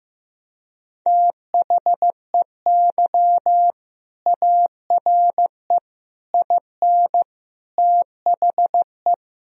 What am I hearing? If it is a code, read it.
Morse code, heard as a keyed tone, about 15 wpm, 700 Hz. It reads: THEY ARE IN THE
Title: Sentences from Top 1000 Words - Review Part 2 - 15wpm